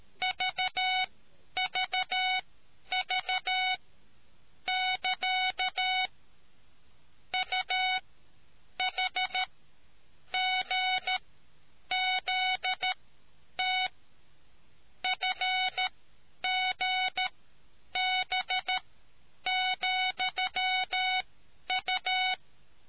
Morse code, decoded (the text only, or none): VVV <KA> UHGZT FGB,U